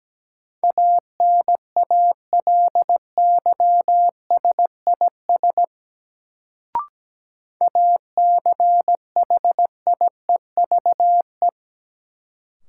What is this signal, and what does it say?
Morse code, heard as a keyed tone, about 17 wpm, 700 Hz. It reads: ANALYSIS ACHIEVE
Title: Sentences from Top 1000 Words - Review Part 5 - New Word Drill - 17wpm